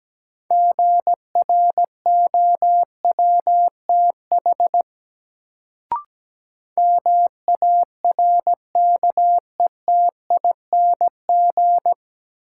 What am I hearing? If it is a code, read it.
Morse code, heard as a keyed tone, about 17 wpm, 700 Hz. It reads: GROWTH MARKETING
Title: Sentences from Top 1000 Words - Review Part 5 - New Word Drill - 17wpm